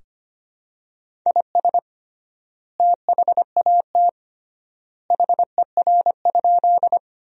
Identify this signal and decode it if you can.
Morse code, heard as a keyed tone, about 25 wpm, 700 Hz. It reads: IS THAT HER?
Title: 3x Word-Spacing - Sentences from Top 100 Words - 25wpm